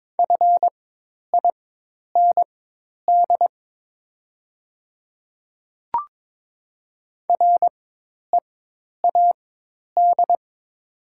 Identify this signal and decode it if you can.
Morse code, heard as a keyed tone, about 22 wpm, spaced as for 10 wpm, 700 Hz. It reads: FIND READ
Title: 4x Character Spacing - Top 300 Words - 22wpm